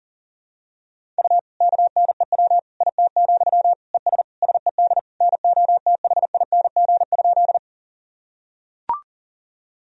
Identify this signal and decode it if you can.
Morse code, heard as a keyed tone, about 37 wpm, 700 Hz. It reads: U KNEW IT, ES SED NOTHING?
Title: Sentences from Top 500 Words and Abbreviations 40wpm